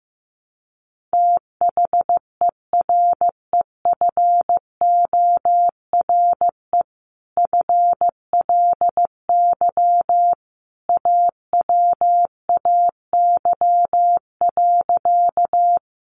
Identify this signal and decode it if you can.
Morse code, heard as a keyed tone, about 15 wpm, 700 Hz. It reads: THEREFORE FLY AWAY.
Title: Sentences from Top 800 Words 15wpm